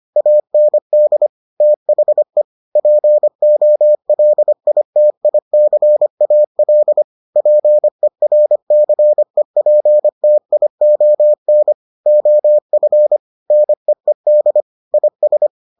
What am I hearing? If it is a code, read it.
Morse code, heard as a keyed tone, about 25 wpm, 600 Hz. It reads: AND THE POLITICAL PERCEPTION OF NEED IS